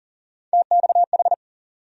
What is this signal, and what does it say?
Morse code, heard as a keyed tone, about 40 wpm, 700 Hz. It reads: TXH
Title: USA Call Sign Postfixes - 3 Letters 40wpm